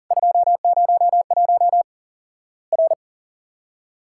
Morse code, heard as a keyed tone, about 40 wpm, 700 Hz. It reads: 201 R
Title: Worked All Europe DX Contest - QTC Exchange - 40wpm